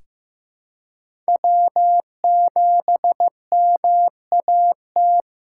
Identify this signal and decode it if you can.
Morse code, heard as a keyed tone, about 16 wpm, 700 Hz. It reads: W7MAT
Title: USA Callsign DE Callsign - 5 Characters - 15wpm